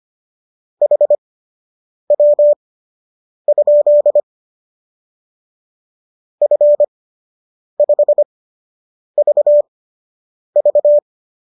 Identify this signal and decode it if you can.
Morse code, heard as a keyed tone, about 25 wpm, spaced as for 8 wpm, 590 Hz. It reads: HW? F5VV